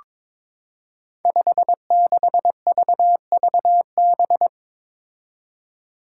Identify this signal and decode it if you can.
Morse code, heard as a keyed tone, about 22 wpm, 700 Hz. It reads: 56VVB
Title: Five Challenging Letter-Number Combinations 22wpm